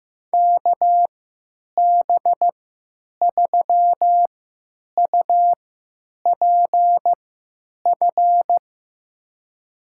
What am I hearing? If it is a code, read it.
Morse code, heard as a keyed tone, about 15 wpm, spaced as for 9 wpm, 700 Hz. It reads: KB3UPF
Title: USA Call Signs - 6 Characters - 3x Character Spacing - 15wpm